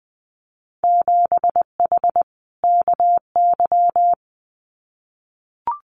Morse code, heard as a keyed tone, about 20 wpm, 700 Hz. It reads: 7H KY
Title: Winter Field Day Exchanges Encore 20wpm